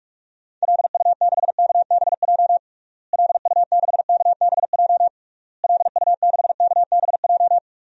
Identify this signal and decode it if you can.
Morse code, heard as a keyed tone, about 41 wpm, 700 Hz. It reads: LU6KBJ LU6KBJ LU6KBJ
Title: DXCC-100 Call Signs - 6 Characters - Repeated 3x - 45wpm